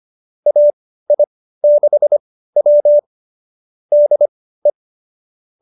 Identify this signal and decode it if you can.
Morse code, heard as a keyed tone, about 25 wpm, spaced as for 15 wpm, 590 Hz. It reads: AI6W DE